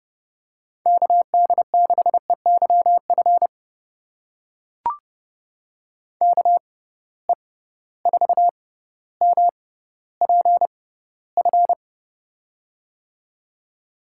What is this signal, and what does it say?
Morse code, heard as a keyed tone, about 32 wpm, spaced as for 16 wpm, 700 Hz. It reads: KD6EYF KE4MPF
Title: USA Call Signs - 6 Characters - 6x Character Spacing - 30wpm